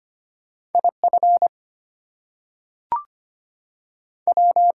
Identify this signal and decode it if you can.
Morse code, heard as a keyed tone, about 25 wpm, 700 Hz. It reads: IF W
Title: Two Letter Words - 25wpm